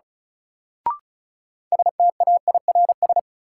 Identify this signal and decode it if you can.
Morse code, heard as a keyed tone, about 36 wpm, 700 Hz. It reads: STAIRS
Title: Sets of 5 Words Encore - New Word Drill - 35wpm